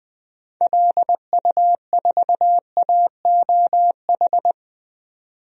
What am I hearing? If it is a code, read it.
Morse code, heard as a keyed tone, about 20 wpm, 700 Hz. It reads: LU4AOH
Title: DXCC-100 Call Signs - 6 Characters - 20wpm